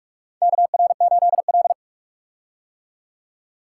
Morse code, heard as a keyed tone, about 45 wpm, 700 Hz. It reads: KR8L
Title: K1USN SST QSO Practice - 45wpm